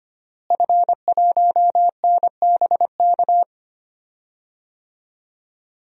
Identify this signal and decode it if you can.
Morse code, heard as a keyed tone, about 25 wpm, 700 Hz. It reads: F1NBK
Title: DXCC-100 Call Signs - 5 Characters - Repeated 2x - 25wpm